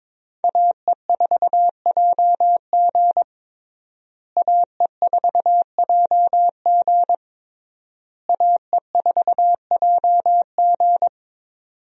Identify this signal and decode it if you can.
Morse code, heard as a keyed tone, about 22 wpm, 700 Hz. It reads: AE4JG AE4JG AE4JG
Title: USA Call Signs - 5 Characters - Repeated 3x - 22wpm